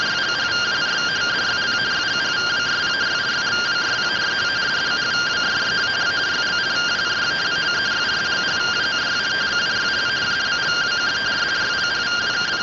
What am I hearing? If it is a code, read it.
Two-tone teleprinter signal, about 170 Hz shift.